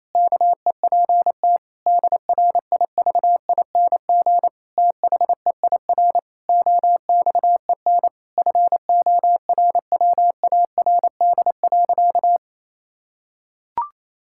Morse code, heard as a keyed tone, about 28 wpm, 700 Hz. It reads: KEPT DRIVING THEIR OXEN FORWARD.
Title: Pangram Challenge v2 - 28wpm